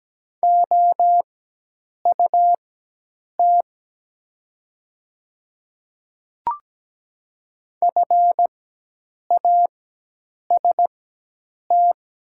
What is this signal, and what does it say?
Morse code, heard as a keyed tone, about 17 wpm, spaced as for 8 wpm, 700 Hz. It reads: OUT FAST
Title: 4x Character Spacing - Top 500 Words - 17wpm